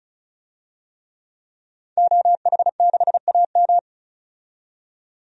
Morse code, heard as a keyed tone, about 34 wpm, 700 Hz. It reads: OH6AM
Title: DXCC-100 Call Signs - 5 Characters - 35wpm